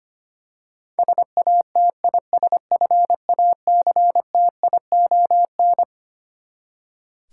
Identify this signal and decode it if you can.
Morse code, heard as a keyed tone, about 25 wpm, 700 Hz. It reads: SATISFACTION
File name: Twelve Letter Words - 25wpm